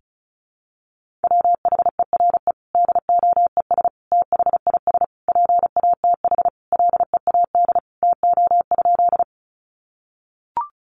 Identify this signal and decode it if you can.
Morse code, heard as a keyed tone, about 34 wpm, 700 Hz. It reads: WHERE DOES THIS PATH LEAD TO?
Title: Sentences from Top 1400 Words 35wpm